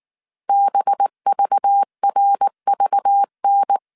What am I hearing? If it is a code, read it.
Morse code, heard as a keyed tone, about 19 wpm, 800 Hz. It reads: BVRVN
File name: CW / DET / 1 6 1500C S A 18wpm 2255